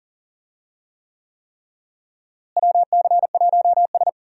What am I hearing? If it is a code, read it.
Morse code, heard as a keyed tone, about 37 wpm, 700 Hz. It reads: WC1S
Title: USA Call Signs - 4 Characters - Repeated 3x - 40wpm